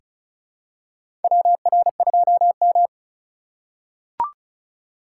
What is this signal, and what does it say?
Morse code, heard as a keyed tone, about 36 wpm, 700 Hz. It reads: WR2M
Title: USA Call Signs - 4 Characters - 4x Character Spacing - 35wpm